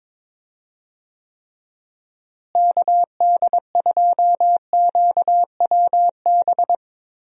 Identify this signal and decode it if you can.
Morse code, heard as a keyed tone, about 22 wpm, 700 Hz. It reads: KD2QWB
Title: USA Call Signs - 6 Characters - Repeated 3x - 22wpm